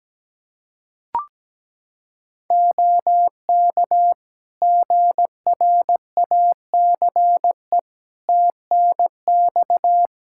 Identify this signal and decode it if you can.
Morse code, heard as a keyed tone, about 17 wpm, 700 Hz. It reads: OK GRACE TNX
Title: Practice QSO - Exchange 4 - Ending - 17wpm